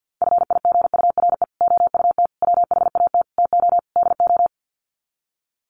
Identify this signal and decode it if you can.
Morse code, heard as a keyed tone, about 50 wpm, 700 Hz. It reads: FIGURE OUT WHAT TO DO
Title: Sets of 5 Words Encore2 50wpm